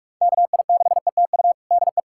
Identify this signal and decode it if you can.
Morse code, heard as a keyed tone, about 45 wpm, 700 Hz. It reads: KI6ETU DE
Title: USA Callsign DE Callsign - 6 Characters - 45wpm